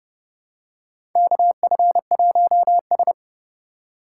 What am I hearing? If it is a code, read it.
Morse code, heard as a keyed tone, about 30 wpm, 700 Hz. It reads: KF1S
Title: USA Call Signs - 4 Characters - 2x Character Spacing - 30wpm